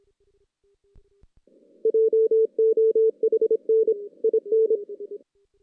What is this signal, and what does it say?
Morse code, heard as a keyed tone, about 27 wpm, 440 Hz. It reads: JOHN IN
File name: 2016 JAN CW NAQP / 2016 Jan naqp k5ot - 0-5